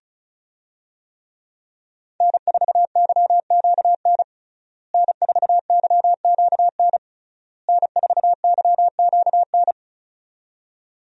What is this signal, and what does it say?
Morse code, heard as a keyed tone, about 36 wpm, 700 Hz. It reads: N4YQN N4YQN N4YQN
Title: USA Call Signs - 5 Characters - Repeated 3x - 35wpm